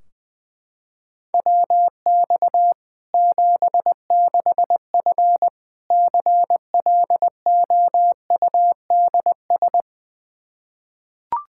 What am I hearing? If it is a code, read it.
Morse code, heard as a keyed tone, about 20 wpm, 700 Hz. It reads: WX 76F CLOUDS
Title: Practice QSO - Exchange 2 - Info - 20wpm